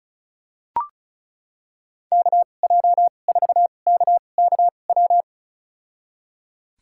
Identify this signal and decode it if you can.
Morse code, heard as a keyed tone, about 34 wpm, spaced as for 25 wpm, 700 Hz. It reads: KJ4KKW
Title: USA Call Signs - 6 Characters - 2x Character Spacing - 35wpm